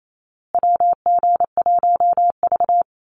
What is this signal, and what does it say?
Morse code, heard as a keyed tone, about 28 wpm, 700 Hz. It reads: WG1V
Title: USA Call Signs Repeated 3x 28wpm